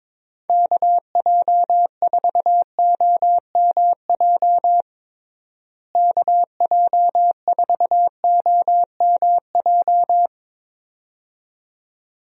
Morse code, heard as a keyed tone, about 22 wpm, 700 Hz. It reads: KJ4OMJ KJ4OMJ
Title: USA Call Signs - 6 Characters - Repeated 2x - 22wpm